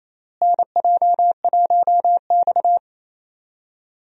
Morse code, heard as a keyed tone, about 28 wpm, 700 Hz. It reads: NJ1X
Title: USA Call Signs - 4 Characters - Repeated 3x - 28wpm